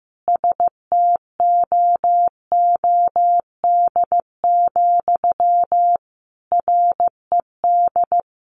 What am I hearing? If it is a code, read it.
Morse code, heard as a keyed tone, about 15 wpm, 700 Hz. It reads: STOOD, RED
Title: A Sound Of Thunder 15wpm